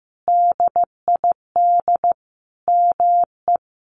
Morse code, heard as a keyed tone, about 15 wpm, 700 Hz. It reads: DID ME
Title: Sentences from Top 1300 Words 15wpm